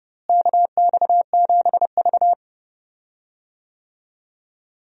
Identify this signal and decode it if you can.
Morse code, heard as a keyed tone, about 30 wpm, 700 Hz. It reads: KX7V
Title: USA Call Signs - 4 Characters - Repeated 3x - 30wpm